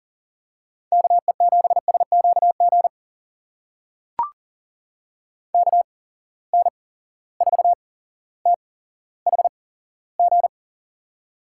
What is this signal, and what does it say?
Morse code, heard as a keyed tone, about 39 wpm, spaced as for 17 wpm, 700 Hz. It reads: KE7SQG KN4THG
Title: USA Call Signs - 6 Characters - 8x Character Spacing - 40wpm